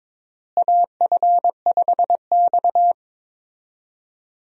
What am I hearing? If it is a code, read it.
Morse code, heard as a keyed tone, about 22 wpm, 700 Hz. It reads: AF5X
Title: USA Call Signs - 4 Characters - Repeated 3x - 22wpm